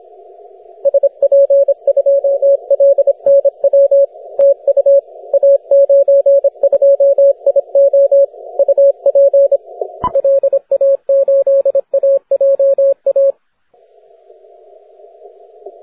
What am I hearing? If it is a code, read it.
Morse code, heard as a keyed tone, about 27 wpm, 560 Hz. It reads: SP2LNW TU A92IO UP LA8AJA
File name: wp / a92io 15m cw 11 dec 2010 0918z